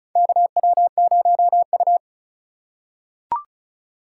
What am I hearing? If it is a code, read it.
Morse code, heard as a keyed tone, about 36 wpm, 700 Hz. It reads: KW0U
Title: USA Call Signs - 4 Characters - 3x Character Spacing - 35wpm